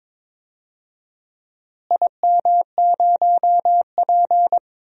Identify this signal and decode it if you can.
Morse code, heard as a keyed tone, about 22 wpm, 700 Hz. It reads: IM0P
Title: DXCC-100 Call Signs - 4 Characters - Repeated 3x - 22wpm